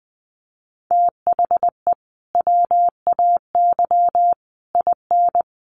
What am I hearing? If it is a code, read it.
Morse code, heard as a keyed tone, about 20 wpm, 700 Hz. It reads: THE WAY IN
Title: Sets of 5 Words Encore2 20wpm